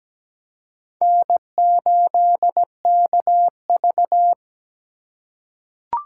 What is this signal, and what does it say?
Morse code, heard as a keyed tone, about 17 wpm, 700 Hz. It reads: N8KV
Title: USA Call Signs - 4 Characters - 6x Character Spacing - 17wpm